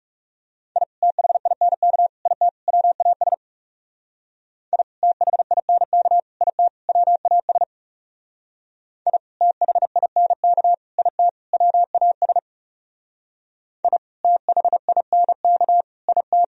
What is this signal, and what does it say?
Morse code, heard as a keyed tone, about 38 wpm, 700 Hz. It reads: I THINK IT WAS I THINK IT WAS I THINK IT WAS I THINK IT